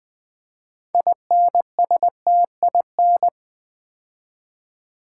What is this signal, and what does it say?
Morse code, heard as a keyed tone, about 20 wpm, 700 Hz. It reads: INSTIN
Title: Build A Word - 3 Letters At A Time - 8-Letter Words - 20wpm